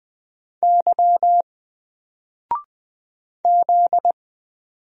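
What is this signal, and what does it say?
Morse code, heard as a keyed tone, about 20 wpm, 700 Hz. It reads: Y Z
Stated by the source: Single letter and number Morse drill at 20 wpm, from N4LCD.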